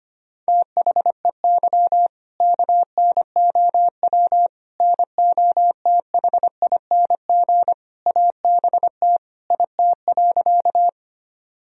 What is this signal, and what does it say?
Morse code, heard as a keyed tone, about 25 wpm, 700 Hz. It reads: THEY KNOW NOTHING ABT IT.
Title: Sentences from Top 500 Words and Abbreviations 25wpm